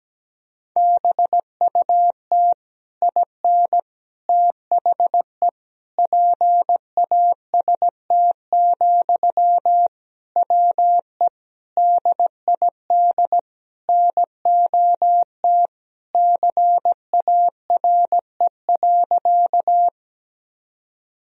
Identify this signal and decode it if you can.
Morse code, heard as a keyed tone, about 17 wpm, 700 Hz. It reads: BUT IN THE PAST, WE DID NOT CARE.
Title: Sentences from Top 1000 Words - Review Part 3 - 17wpm